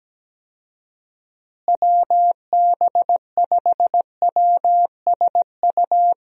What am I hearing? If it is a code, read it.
Morse code, heard as a keyed tone, about 17 wpm, 700 Hz. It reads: WB5WSU
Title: USA Call Signs - 6 Characters - Repeated 2x - 17wpm